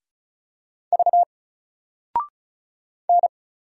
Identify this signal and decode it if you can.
Morse code, heard as a keyed tone, about 36 wpm, 700 Hz. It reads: V N
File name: Single-Letter-Number-35wpm